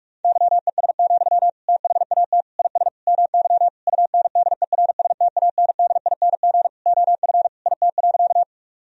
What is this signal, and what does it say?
Morse code, heard as a keyed tone, about 41 wpm, 700 Hz. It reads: YES, THAT IS MY UNDERSTANDING OF IT.
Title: Sentences from Top 1000 Words - Review Part 3 - 45wpm